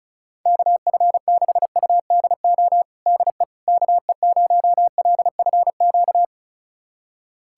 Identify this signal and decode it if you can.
Morse code, heard as a keyed tone, about 34 wpm, 700 Hz. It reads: KF6UDO DE KE0LFQ
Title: USA Callsign DE Callsign 35wpm